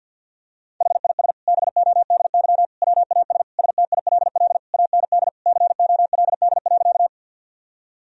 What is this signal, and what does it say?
Morse code, heard as a keyed tone, about 50 wpm, 700 Hz. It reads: HIS BODY WAS STILL AND COLD.
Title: Sentences from Top 1000 Words - Review Part 2 - 50wpm